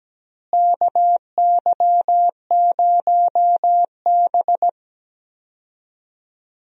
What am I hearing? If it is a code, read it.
Morse code, heard as a keyed tone, about 17 wpm, 700 Hz. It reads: KY0B
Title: USA Call Signs - 4 Characters - Repeated 3x - 17wpm